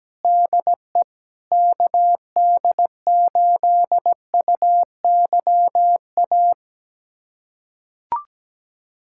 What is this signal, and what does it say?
Morse code, heard as a keyed tone, about 17 wpm, 700 Hz. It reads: DE KD8UYA
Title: USA Callsign DE Callsign - 6 Characters - 17wpm